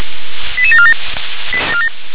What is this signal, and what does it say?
Five-tone selective call (ZVEI), heard as quick runs of five tones, tones of about 80 ms. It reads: 88747